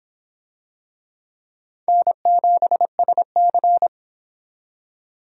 Morse code, heard as a keyed tone, about 25 wpm, 700 Hz. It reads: N7SC